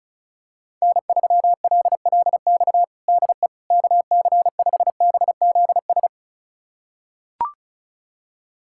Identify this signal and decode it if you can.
Morse code, heard as a keyed tone, about 36 wpm, 700 Hz. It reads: N3LLX DE KC5BZS
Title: USA Callsign DE Callsign 35wpm